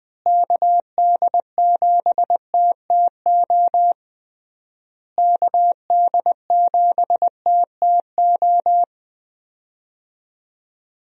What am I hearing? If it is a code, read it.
Morse code, heard as a keyed tone, about 20 wpm, 700 Hz. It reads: KD7TTO KD7TTO
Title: USA Call Signs - 6 Characters - Repeated 3x - 20wpm